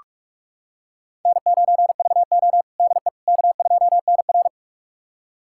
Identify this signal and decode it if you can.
Morse code, heard as a keyed tone, about 46 wpm, 700 Hz. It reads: N9VO DE K2NR
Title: USA Callsign DE Callsign - 4 Characters - 45wpm